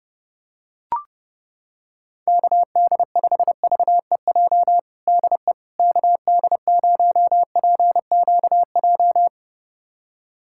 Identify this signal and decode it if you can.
Morse code, heard as a keyed tone, about 32 wpm, 700 Hz. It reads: KD5VEJ DE KD0PQJ
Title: USA Callsign DE Callsign - 6 Characters - 30wpm